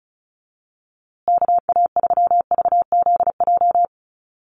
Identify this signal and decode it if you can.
Morse code, heard as a keyed tone, about 34 wpm, 700 Hz. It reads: KA3VZJ
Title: USA Call Signs Repeated 3x 35wpm